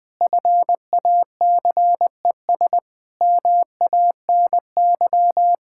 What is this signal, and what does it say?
Morse code, heard as a keyed tone, about 20 wpm, 700 Hz. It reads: FACES MANY
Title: Sentences with 7-Word Phrases - Spoken Beginning - 20wpm